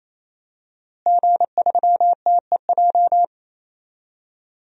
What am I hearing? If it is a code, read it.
Morse code, heard as a keyed tone, about 28 wpm, 700 Hz. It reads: G3TEJ
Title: DXCC-100 Call Signs - 5 Characters - Repeated 3x - 28wpm